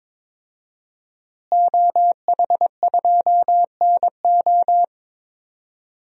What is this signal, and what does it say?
Morse code, heard as a keyed tone, about 22 wpm, 700 Hz. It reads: OH2NO
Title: DXCC-100 Call Signs - 5 Characters - Repeated 2x - 22wpm